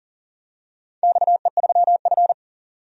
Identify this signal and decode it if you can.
Morse code, heard as a keyed tone, about 42 wpm, 700 Hz. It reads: XE3F
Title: DXCC-100 Call Signs - 4 Characters - 40wpm